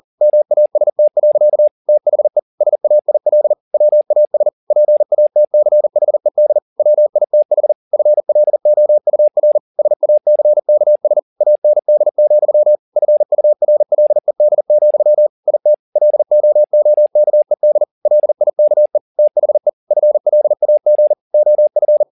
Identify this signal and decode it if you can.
Morse code, heard as a keyed tone, about 40 wpm, 600 Hz. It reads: MAST. THE SAIL WAS PATCHED WITH FLOUR SACKS AND, FURLED, IT LOOKED LIKE THE FLAG OF
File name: the old man and the sea 1 40wpm